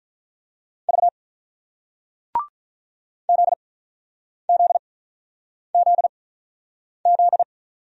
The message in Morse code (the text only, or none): V ZZZZ